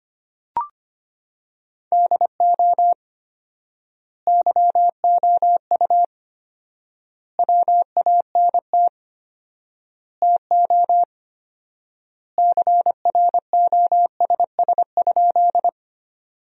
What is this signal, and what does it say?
Morse code, heard as a keyed tone, about 25 wpm, 700 Hz. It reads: DO YOU WANT TO CROSS?
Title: 4x Word-Spacing - Sentences from Top 300 Words - 25wpm